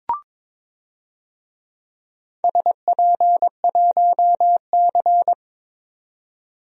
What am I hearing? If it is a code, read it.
Morse code, heard as a keyed tone, about 22 wpm, 700 Hz. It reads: SP1C
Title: DXCC-100 Call Signs - 4 Characters - 22wpm